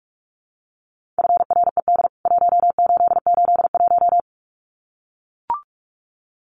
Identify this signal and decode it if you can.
Morse code, heard as a keyed tone, about 44 wpm, 700 Hz. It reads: FRED 1881
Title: CWT Exchange 45wpm